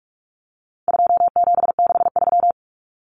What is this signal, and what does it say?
Morse code, heard as a keyed tone, about 44 wpm, 700 Hz. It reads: 2763
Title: Common Cut Numbers - 4-Digit 45wpm